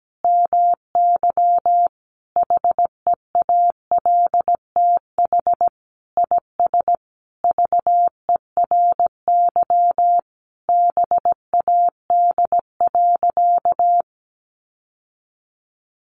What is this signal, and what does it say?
Morse code, heard as a keyed tone, about 17 wpm, 700 Hz. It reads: MY HEALTH IS VERY BAD.
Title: Sentences from Top 1200 Words 17wpm